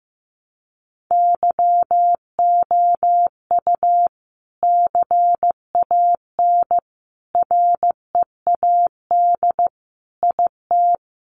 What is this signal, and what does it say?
Morse code, heard as a keyed tone, about 15 wpm, 700 Hz. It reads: YOU CAN READ IT
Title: Sentences from Top 600 Words 15wpm